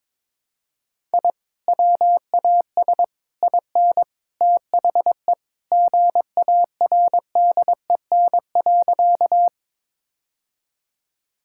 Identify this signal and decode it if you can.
Morse code, heard as a keyed tone, about 22 wpm, 700 Hz. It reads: I WAS IN THE GARDEN.
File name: practice / 22wpm / Sentences with 3-Word Phrases - 22wpm